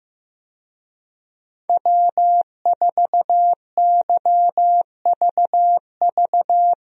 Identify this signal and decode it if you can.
Morse code, heard as a keyed tone, about 15 wpm, 700 Hz. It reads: W4YVV
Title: USA Call Signs - 5 Characters - Repeated 2x - 15wpm